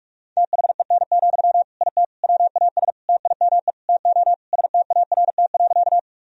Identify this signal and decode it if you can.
Morse code, heard as a keyed tone, about 46 wpm, 700 Hz. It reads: THEN, IT WAS TIME TO START.
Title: Sentences from Top 1000 Words - Review Part 1 - 45wpm